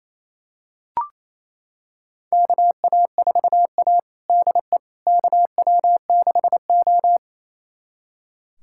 Morse code, heard as a keyed tone, about 28 wpm, 700 Hz. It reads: KA4A DE KW6O
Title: USA Callsign DE Callsign - 4 Characters - 28wpm